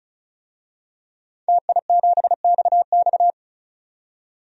Morse code, heard as a keyed tone, about 36 wpm, 700 Hz. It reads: TI7XX
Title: DXCC-100 Call Signs - 5 Characters - 35wpm